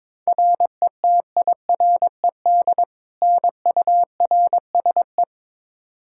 Morse code, heard as a keyed tone, about 22 wpm, 700 Hz. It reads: RETIRED NURSE
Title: Practice QSO - Exchange 3 - More Info - 22wpm